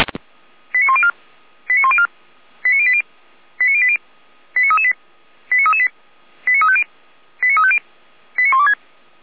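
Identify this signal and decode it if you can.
Five-tone selective call (ZVEI), heard as quick runs of five tones, tones of about 70 ms. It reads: 89104 89104 89080 89080 89338 89338 89350 89350 89126
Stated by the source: ZVEI five-tone call 89303 received by SDR.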